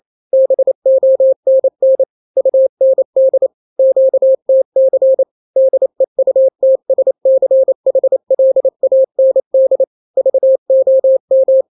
Morse code, heard as a keyed tone, about 28 wpm, 530 Hz. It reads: BONN UND QTC DEUTSCHLAND VOM
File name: daily / 27wpm